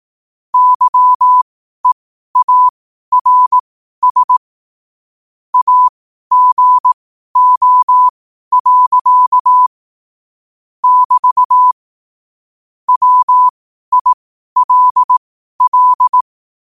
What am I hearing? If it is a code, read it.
Morse code, heard as a keyed tone, about 18 wpm, spaced as for 12 wpm, 1000 Hz. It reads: YEARS AGO. <BT> WILL
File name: qotdfiles / 12wpm